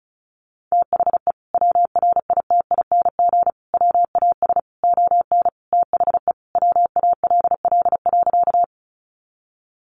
Morse code, heard as a keyed tone, about 34 wpm, 700 Hz. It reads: THE WRITING WAS ON THE WALL.
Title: Sentences from Top 1100 Words 35wpm